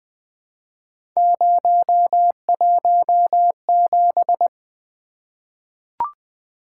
Morse code, heard as a keyed tone, about 20 wpm, 700 Hz. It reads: 017
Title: Three Letter-Number Combinations - 20wpm